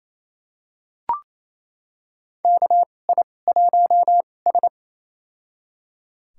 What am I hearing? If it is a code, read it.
Morse code, heard as a keyed tone, about 28 wpm, spaced as for 20 wpm, 700 Hz. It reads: KI1S
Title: USA Call Signs - 4 Characters - 2x Character Spacing - 28wpm